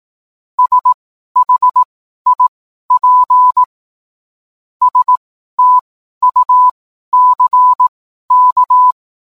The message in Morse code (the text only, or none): SHIP STUCK